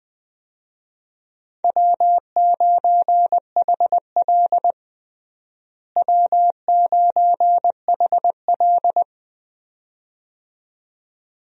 Morse code, heard as a keyed tone, about 20 wpm, 700 Hz. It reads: W9HL W9HL
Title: USA Call Signs - 4 Characters - Repeated 2x - 20wpm